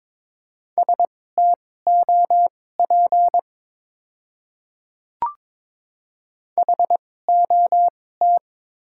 Morse code, heard as a keyed tone, about 21 wpm, spaced as for 16 wpm, 700 Hz. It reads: STOP HOT
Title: 2x Character Spacing - Top 500 Words - 22wpm